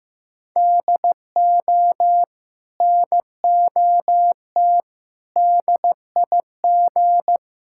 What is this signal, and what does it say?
Morse code, heard as a keyed tone, about 16 wpm, 700 Hz. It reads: DO NOT DIG
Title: Sentences from Top 1600 Words 15wpm